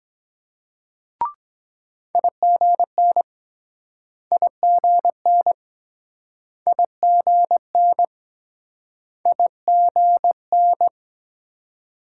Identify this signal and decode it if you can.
Morse code, heard as a keyed tone, about 23 wpm, 700 Hz. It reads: IGN IGN IGN IGN